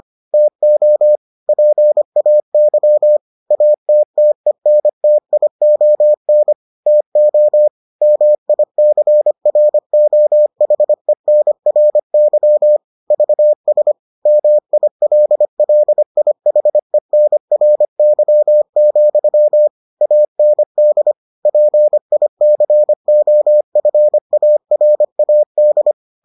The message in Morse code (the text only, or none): TO PAY ATTENTION TO MICROHENRY VS MILLIHENRY, AND PICOFARAD